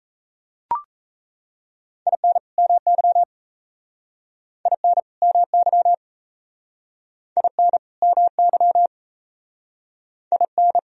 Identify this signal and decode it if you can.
Morse code, heard as a keyed tone, about 35 wpm, 700 Hz. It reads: IN MY IN MY IN MY IN